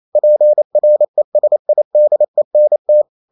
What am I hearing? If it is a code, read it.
Morse code, heard as a keyed tone, about 28 wpm, 600 Hz. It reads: PRESIDENT
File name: Never 0027wpm